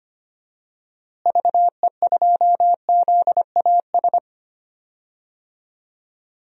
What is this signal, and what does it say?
Morse code, heard as a keyed tone, about 25 wpm, 700 Hz. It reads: VE2ZAS
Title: DXCC-100 Call Signs - 6 Characters - Repeated 2x - 25wpm